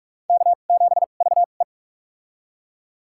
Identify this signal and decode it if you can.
Morse code, heard as a keyed tone, about 46 wpm, spaced as for 33 wpm, 700 Hz. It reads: K7VE
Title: USA Call Signs - 2x Character Spacing - 45wpm